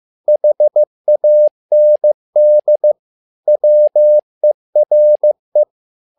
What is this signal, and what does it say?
Morse code, heard as a keyed tone, about 15 wpm, 600 Hz. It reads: HAND WERE
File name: learning center / tower trouble / tower trouble 15wpm